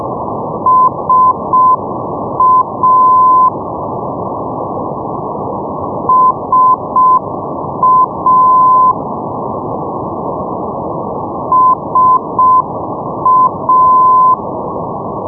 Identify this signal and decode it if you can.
Morse code, heard as a keyed tone, about 6 wpm, 1020 Hz, with noise. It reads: SA SA SA
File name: europe / pol / pol-531 ndb darlowo sa 474.5 (rug) 2008 rb